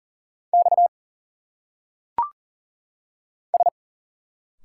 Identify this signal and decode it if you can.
Morse code, heard as a keyed tone, about 38 wpm, 700 Hz. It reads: X S
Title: Single Letters - Phonetic Pronunciation 40wpm